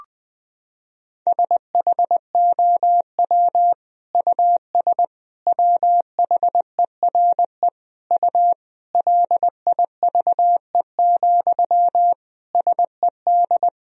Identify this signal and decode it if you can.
Morse code, heard as a keyed tone, about 20 wpm, 700 Hz. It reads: SHOW US WHERE U LIVE, SED
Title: Sentences from Top 200 Words and Abbreviations 20wpm